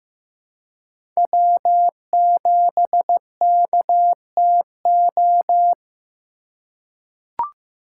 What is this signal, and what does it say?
Morse code, heard as a keyed tone, about 15 wpm, 700 Hz. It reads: W7KTO
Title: USA Call Signs - 5 Characters - 2x Character Spacing - 15wpm